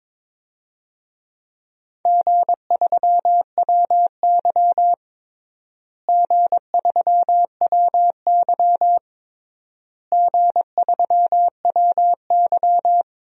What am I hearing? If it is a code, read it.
Morse code, heard as a keyed tone, about 21 wpm, 700 Hz. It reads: G3WY G3WY G3WY
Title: DXCC-100 Call Signs - 4 Characters - Repeated 3x - 22wpm